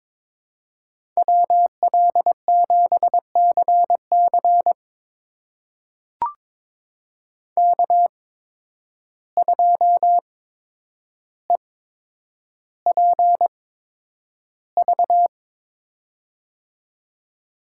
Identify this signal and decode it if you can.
Morse code, heard as a keyed tone, about 22 wpm, spaced as for 9 wpm, 700 Hz. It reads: WL7CC K2EPV
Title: USA Call Signs - 8x Character Spacing - 22wpm